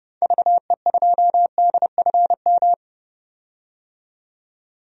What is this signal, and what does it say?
Morse code, heard as a keyed tone, about 30 wpm, 700 Hz. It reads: VE2DFM
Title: DXCC-100 Call Signs - 6 Characters - 30wpm